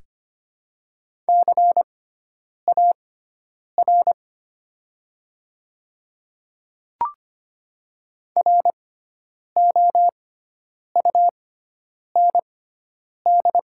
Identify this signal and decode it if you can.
Morse code, heard as a keyed tone, about 25 wpm, spaced as for 9 wpm, 700 Hz. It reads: CAR ROUND
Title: 6x Character Spacing - Top 300 Words - 25wpm